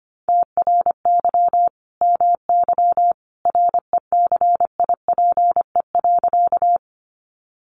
Morse code, heard as a keyed tone, about 25 wpm, 700 Hz. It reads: TRY MY RECIPE.
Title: Sentences from Top 1200 Words 25wpm